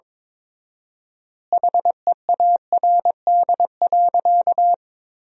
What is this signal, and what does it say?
Morse code, heard as a keyed tone, about 22 wpm, 700 Hz. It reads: HEARD.
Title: 4x Word-Spacing - Sentences from Top 400 Words - 22wpm